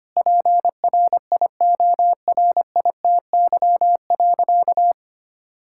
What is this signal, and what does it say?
Morse code, heard as a keyed tone, about 25 wpm, 700 Hz. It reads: PRIORITY.
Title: Sentences from Top 1600 Words 25wpm